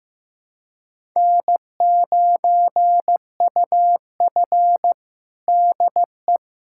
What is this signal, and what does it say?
Morse code, heard as a keyed tone, about 15 wpm, 700 Hz. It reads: N9UF DE
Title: USA Callsign DE Callsign - 4 Characters - 15wpm